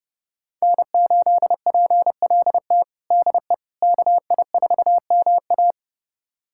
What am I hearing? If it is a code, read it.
Morse code, heard as a keyed tone, about 32 wpm, 700 Hz. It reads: N8PLT DE KI4MA
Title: USA Callsign DE Callsign - 5 Characters - 30wpm